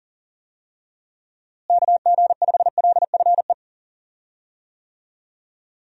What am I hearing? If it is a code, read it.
Morse code, heard as a keyed tone, about 40 wpm, 700 Hz. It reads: KG5LFE
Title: USA Call Signs - 6 Characters - Repeated 3x - 40wpm